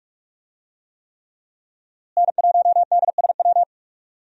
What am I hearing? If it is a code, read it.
Morse code, heard as a keyed tone, about 45 wpm, 700 Hz. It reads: N1DSW
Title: USA Call Signs - 5 Characters - Repeated 3x - 45wpm